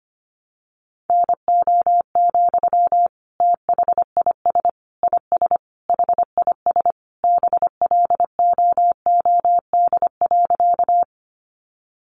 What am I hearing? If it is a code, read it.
Morse code, heard as a keyed tone, about 25 wpm, 700 Hz. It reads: NO, THIS IS HIS BLOOD.
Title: Sentences from Top 1100 Words 25wpm